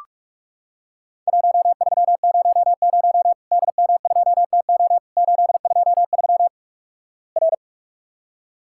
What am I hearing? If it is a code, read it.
Morse code, heard as a keyed tone, about 46 wpm, 700 Hz. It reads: 1300 DM2TO 823 R
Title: Worked All Europe DX Contest - QTC Exchange - 45wpm